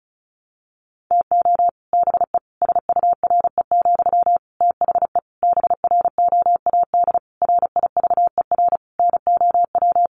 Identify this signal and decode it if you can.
Morse code, heard as a keyed tone, about 34 wpm, 700 Hz. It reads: TO BE SURE, THE BROAD RIVER NOW
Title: Sentences from Top 1000 Words 35wpm